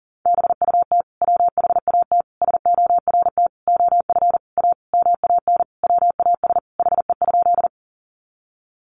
Morse code, heard as a keyed tone, about 40 wpm, 700 Hz. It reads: BUT WHAT SORT OF A MAN WAS HE?
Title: Sentences from Top 1300 Words 40wpm